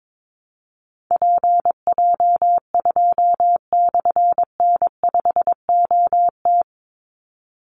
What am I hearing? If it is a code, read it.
Morse code, heard as a keyed tone, about 22 wpm, 700 Hz. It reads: PJ2/N5OT
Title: Call Signs - 22wpm